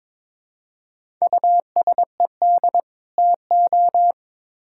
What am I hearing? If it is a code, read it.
Morse code, heard as a keyed tone, about 22 wpm, 700 Hz. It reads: USED TO
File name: Build A Phrase - 2-Word Phrases - 22wpm